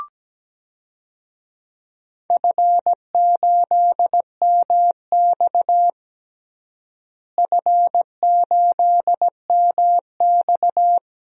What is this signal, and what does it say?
Morse code, heard as a keyed tone, about 17 wpm, 700 Hz. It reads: F8MX F8MX
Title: DXCC-100 Call Signs - 4 Characters - Repeated 3x - 17wpm